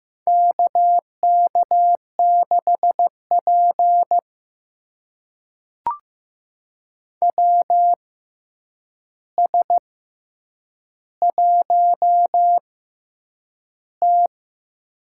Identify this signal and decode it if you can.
Morse code, heard as a keyed tone, about 15 wpm, spaced as for 8 wpm, 700 Hz. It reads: KK6P WS1T
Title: USA Call Signs - 4 Characters - 6x Character Spacing - 15wpm